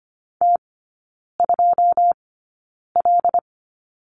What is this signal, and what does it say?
Morse code, heard as a keyed tone, about 25 wpm, spaced as for 9 wpm, 690 Hz. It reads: T2L
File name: Lesson 15e - Number 5 - Three Characters 25wpm